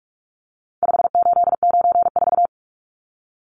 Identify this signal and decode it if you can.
Morse code, heard as a keyed tone, about 44 wpm, 700 Hz. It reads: 5894
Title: Numbers - 4-Digit 45wpm